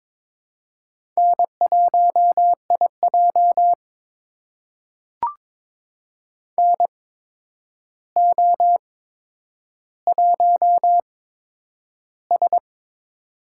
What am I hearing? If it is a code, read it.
Morse code, heard as a keyed tone, about 22 wpm, spaced as for 9 wpm, 700 Hz. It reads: N1IJ NO1S
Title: USA Call Signs - 4 Characters - 8x Character Spacing - 22wpm